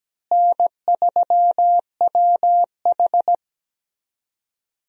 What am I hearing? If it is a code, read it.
Morse code, heard as a keyed tone, about 17 wpm, 700 Hz. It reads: N3WH
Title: USA Call Signs - 4 Characters - Repeated 3x - 17wpm